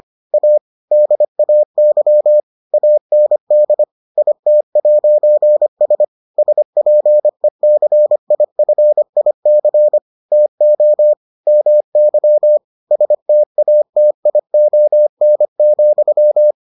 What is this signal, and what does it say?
Morse code, heard as a keyed tone, about 25 wpm, 600 Hz. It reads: A DAY AND IT'S SPECIFIC TO MY STATION,